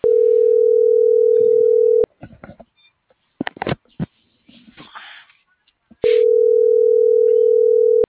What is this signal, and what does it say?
Ringback tone (North America), 440 + 480 Hz.